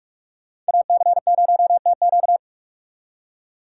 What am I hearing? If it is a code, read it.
Morse code, heard as a keyed tone, about 60 wpm, spaced as for 53 wpm, 700 Hz. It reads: AC0TQ
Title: Speed Racing - USA 2x2 Call Signs 45wpm